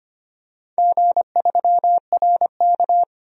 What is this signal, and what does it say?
Morse code, heard as a keyed tone, about 25 wpm, 700 Hz. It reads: G3RK
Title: DXCC-100 Call Signs - 4 Characters - 25wpm